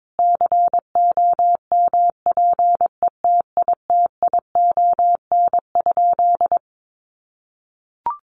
Morse code, heard as a keyed tone, about 22 wpm, 700 Hz. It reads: COMPETITION?
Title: Sentences from Top 1000 Words 22wpm